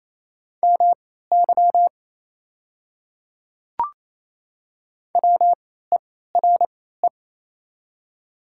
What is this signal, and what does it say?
Morse code, heard as a keyed tone, about 28 wpm, spaced as for 16 wpm, 700 Hz. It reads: MY WERE
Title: 3x Character Spacing - Top 100 Words - 28wpm